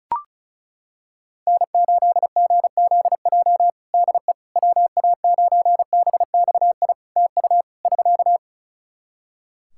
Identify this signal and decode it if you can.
Morse code, heard as a keyed tone, about 36 wpm, 700 Hz. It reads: N8GZJ DE WA9BXI TU <SK>
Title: K1USN SST QSO Practice 35wpm